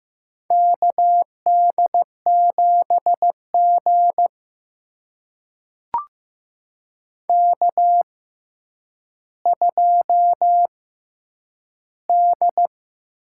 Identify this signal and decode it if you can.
Morse code, heard as a keyed tone, about 15 wpm, spaced as for 9 wpm, 700 Hz. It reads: KD7G K2D
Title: USA Call Signs - 6x Character Spacing - 15wpm